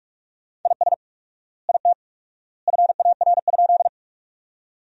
Morse code, heard as a keyed tone, about 46 wpm, 700 Hz. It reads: IS IT FAR?
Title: 4x Word-Spacing - Sentences from Top 300 Words - 45wpm